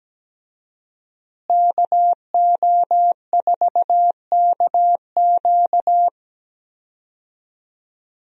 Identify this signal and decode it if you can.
Morse code, heard as a keyed tone, about 17 wpm, 700 Hz. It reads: KO4KQ
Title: USA Call Signs - 5 Characters - Repeated 3x - 17wpm